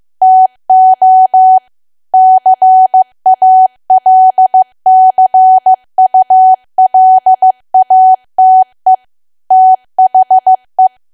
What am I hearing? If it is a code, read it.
Morse code, heard as a keyed tone, about 15 wpm, 750 Hz. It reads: TO CALCULATE THE
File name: cwa / W1AW-Level2 / 15WPM